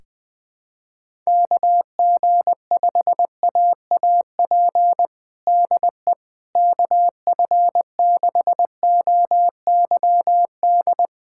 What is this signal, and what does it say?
Morse code, heard as a keyed tone, about 20 wpm, 700 Hz. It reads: KG5AAP DE KF6OYD
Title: USA Callsign DE Callsign - 6 Characters - 20wpm